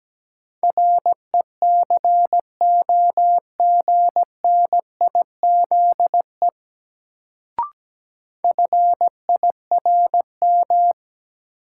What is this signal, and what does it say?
Morse code, heard as a keyed tone, about 17 wpm, 700 Hz. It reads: RECOGNIZE FIRM
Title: Sentences from Top 1400 Words - New Word Drill Alt - 17wpm